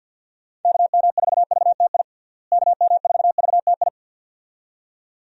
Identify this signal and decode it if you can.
Morse code, heard as a keyed tone, about 47 wpm, 700 Hz. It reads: KM4VTI KM4VTI
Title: USA Call Signs - 6 Characters - Repeated 2x - 50wpm